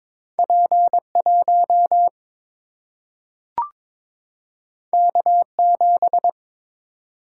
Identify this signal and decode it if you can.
Morse code, heard as a keyed tone, about 22 wpm, 700 Hz. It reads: P 1 K 7
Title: Single-Letter Single-Number 22wpm